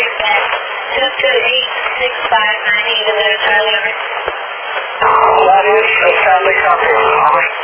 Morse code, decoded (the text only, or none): none